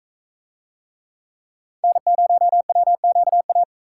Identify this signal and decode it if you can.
Morse code, heard as a keyed tone, about 44 wpm, 700 Hz. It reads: N0WQA